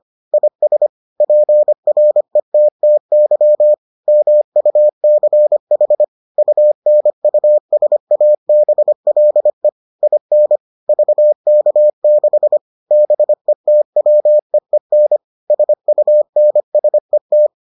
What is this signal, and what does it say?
Morse code, heard as a keyed tone, about 25 wpm, 600 Hz. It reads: IS PRETTY MUCH UNUSABLE IN VK6 BETWEEN SUNSET